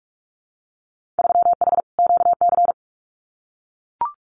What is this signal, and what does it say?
Morse code, heard as a keyed tone, about 44 wpm, 700 Hz. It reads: 3H QC
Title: Winter Field Day Exchanges Encore 45wpm